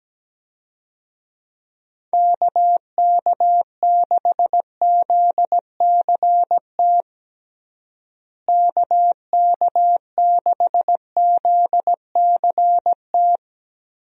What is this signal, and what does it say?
Morse code, heard as a keyed tone, about 17 wpm, 700 Hz. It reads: KK6ZCT KK6ZCT
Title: USA Call Signs - 6 Characters - Repeated 3x - 17wpm